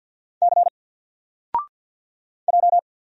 Morse code, heard as a keyed tone, about 54 wpm, 700 Hz. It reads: C J
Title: Single Character (Letter:Number:Punctuation:Prosign) 50wpm